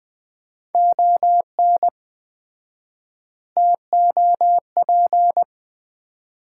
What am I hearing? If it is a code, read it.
Morse code, heard as a keyed tone, about 20 wpm, 690 Hz. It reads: ON TOP
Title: 4x Word-Spacing - Sets of 3 Words - 20wpm